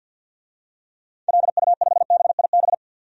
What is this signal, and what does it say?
Morse code, heard as a keyed tone, about 51 wpm, 700 Hz. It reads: LU5BIB
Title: DXCC-100 Call Signs - 6 Characters - 50wpm